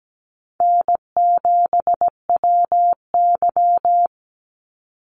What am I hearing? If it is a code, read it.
Morse code, heard as a keyed tone, about 17 wpm, 700 Hz. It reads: N7WY
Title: Easy Call Signs - 17wpm